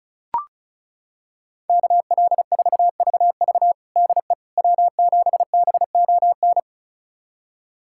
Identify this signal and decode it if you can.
Morse code, heard as a keyed tone, about 36 wpm, 700 Hz. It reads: KL4VV DE W7BON
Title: USA Callsign DE Callsign - 5 Characters - 35wpm